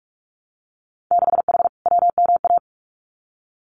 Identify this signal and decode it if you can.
Morse code, heard as a keyed tone, about 44 wpm, 700 Hz. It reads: 6H WMA
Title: Winter Field Day Exchanges Encore 45wpm